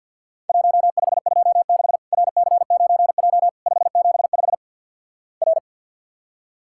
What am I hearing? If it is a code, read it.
Morse code, heard as a keyed tone, about 50 wpm, 700 Hz. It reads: 1526 RC9J 575 R
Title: Worked All Europe DX Contest - QTC Exchange - 50wpm